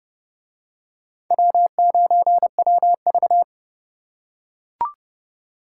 Morse code, heard as a keyed tone, about 27 wpm, 700 Hz. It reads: W9WV